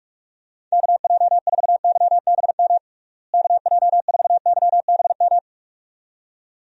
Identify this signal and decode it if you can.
Morse code, heard as a keyed tone, about 48 wpm, 700 Hz. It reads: KJ4YBM KJ4YBM
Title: USA Call Signs - 6 Characters - Repeated 2x - 45wpm